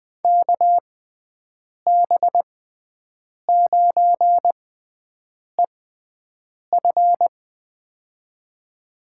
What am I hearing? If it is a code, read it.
Morse code, heard as a keyed tone, about 20 wpm, spaced as for 7 wpm, 700 Hz. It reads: KB9EF
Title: USA Call Signs - 5 Characters - 6x Character Spacing - 20wpm